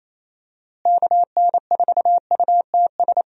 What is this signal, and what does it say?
Morse code, heard as a keyed tone, about 27 wpm, 700 Hz. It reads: KN4UTS
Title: USA Call Signs - 6 Characters - Repeated 3x - 28wpm